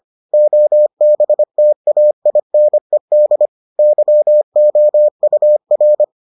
Morse code, heard as a keyed tone, about 25 wpm, 600 Hz. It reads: OBTAINED YOUR